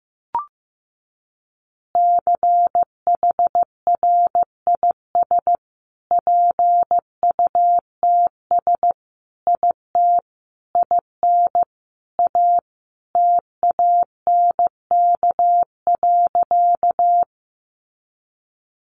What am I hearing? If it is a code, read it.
Morse code, heard as a keyed tone, about 15 wpm, 700 Hz. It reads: CHRIS PUTS IT IN A TANK.
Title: Additional Easy Phrases 15wpm